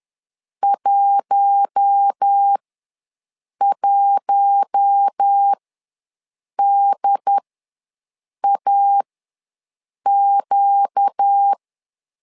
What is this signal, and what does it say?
Morse code, heard as a keyed tone, about 10 wpm, spaced as for 6 wpm, 800 Hz. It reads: 11DAQ
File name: LEZIONI / 4 Q1 ws 10wpm 2538